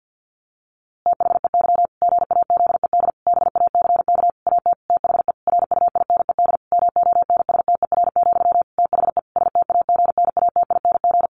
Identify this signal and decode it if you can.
Morse code, heard as a keyed tone, about 50 wpm, 700 Hz. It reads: THEY GAZED BACK AT THE RUINED MONSTER, THE STAGNATING
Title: A Sound Of Thunder 50wpm